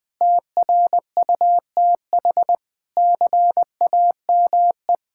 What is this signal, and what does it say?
Morse code, heard as a keyed tone, about 20 wpm, 700 Hz. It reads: TRUTH CAME
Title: Sentences with 9-Word Phrases - 20wpm